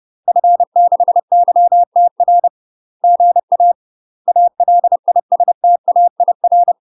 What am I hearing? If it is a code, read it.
Morse code, heard as a keyed tone, about 30 wpm, 700 Hz. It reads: F6YTR GA ALISTAIR
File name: UK MASTER QSOs TRAINING / 30WPM / UK MASTER QSOs 0003